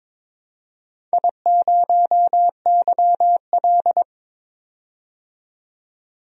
Morse code, heard as a keyed tone, about 22 wpm, 700 Hz. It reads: I0YL
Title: DXCC-100 Call Signs - 4 Characters - 22wpm